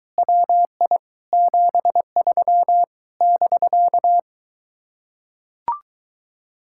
Morse code, heard as a keyed tone, about 22 wpm, 700 Hz. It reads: WI 73 <BK>